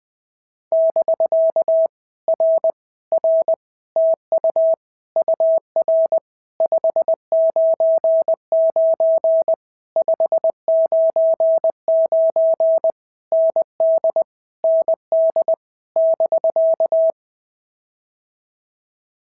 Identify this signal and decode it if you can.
Morse code, heard as a keyed tone, about 19 wpm, 650 Hz. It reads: <BK> R R TU UR 599 599 ND ND <BK>
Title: POTA QSO Practice - 20wpm